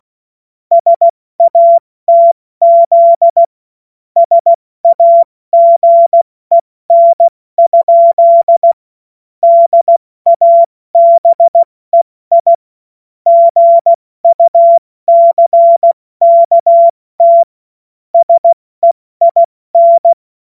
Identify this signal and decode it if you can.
Morse code, heard as a keyed tone, about 16 wpm, 680 Hz. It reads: SATZ SAGEN? DABEI GUCKT SEIN
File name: Morsekurs / Witze / Witz 10. August 2021, 70 F80